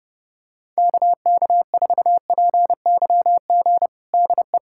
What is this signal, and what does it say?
Morse code, heard as a keyed tone, about 30 wpm, 700 Hz. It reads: KK4PYG DE
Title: USA Callsign DE Callsign 30wpm